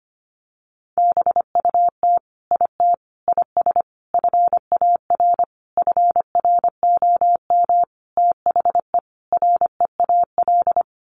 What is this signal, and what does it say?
Morse code, heard as a keyed tone, about 25 wpm, 700 Hz. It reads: BUT IT IS FAR FROM THE REAL
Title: Sentences from Top 1400 Words 25wpm